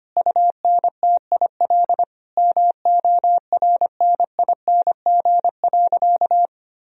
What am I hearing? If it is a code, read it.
Morse code, heard as a keyed tone, about 25 wpm, 700 Hz. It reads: UNTIL MORNING.
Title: Sentences from Top 1000 Words - Review Part 1 - 25wpm